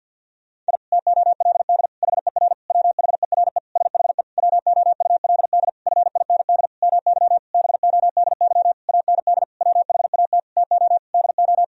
Morse code, heard as a keyed tone, about 52 wpm, 700 Hz. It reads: I TOLD HER WHERE SHE WOULD FIND MY BODY AND WHAT TO DO
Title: Sentences from Top 1000 Words - Review Part 2 - 50wpm